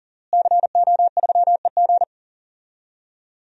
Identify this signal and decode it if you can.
Morse code, heard as a keyed tone, about 40 wpm, 700 Hz. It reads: CO3EG
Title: DXCC-100 Call Signs - 5 Characters - Repeated 2x - 40wpm